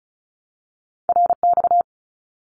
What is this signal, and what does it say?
Morse code, heard as a keyed tone, about 34 wpm, 700 Hz. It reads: RX
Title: Abbreviations Repeated 3x - 35wpm